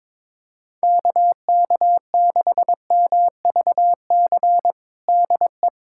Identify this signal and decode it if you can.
Morse code, heard as a keyed tone, about 21 wpm, 700 Hz. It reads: KK6MVC DE
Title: USA Callsign DE Callsign - 6 Characters - 22wpm